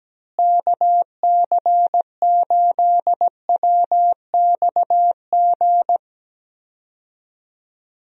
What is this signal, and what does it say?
Morse code, heard as a keyed tone, about 17 wpm, 700 Hz. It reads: KC8WXG
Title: USA Call Signs - 6 Characters - Repeated 2x - 17wpm